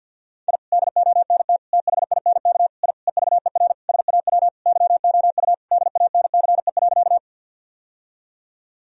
Morse code, heard as a keyed tone, about 51 wpm, 700 Hz. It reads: I DONT THINK I EVER SAW YOU DANCE.
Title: Sentences from Top 1000 Words - Review Part 4 - 50wpm